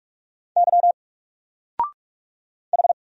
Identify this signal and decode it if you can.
Morse code, heard as a keyed tone, about 44 wpm, 700 Hz. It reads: Y H
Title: Single Character (Letter:Number:Punctuation:Prosign) 45wpm